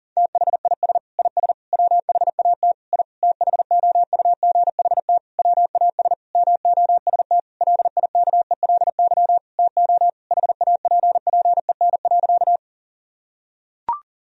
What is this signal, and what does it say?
Morse code, heard as a keyed tone, about 37 wpm, 700 Hz. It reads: THIS IS WHAT I THOUGHT WAS MOST LIKELY TO HAPPEN.
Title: Sentences from Top 1000 Words - Review Part 4 - 40wpm